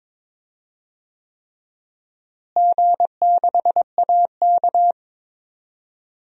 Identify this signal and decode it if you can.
Morse code, heard as a keyed tone, about 21 wpm, 700 Hz. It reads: G6AK
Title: DXCC-100 Call Signs - 4 Characters - Repeated 3x - 22wpm